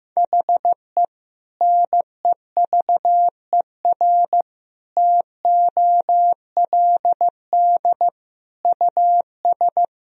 Morse code, heard as a keyed tone, about 15 wpm, 700 Hz. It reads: HE NEVER TOLD US
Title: Sentences from Top 400 Words and Abbreviations 15wpm